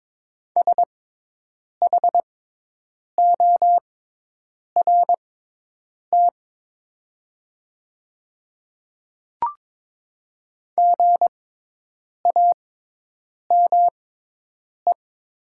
Morse code, heard as a keyed tone, about 22 wpm, spaced as for 8 wpm, 700 Hz. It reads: SHORT GAME
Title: 6x Character Spacing - Top 500 Words - 22wpm